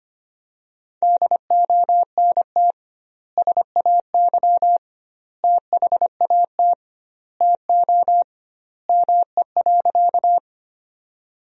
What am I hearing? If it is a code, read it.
Morse code, heard as a keyed tone, about 25 wpm, 700 Hz. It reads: DONT SAY THAT TO ME.
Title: 2x Word-Spacing - Sentences from Top 300 Words - 25wpm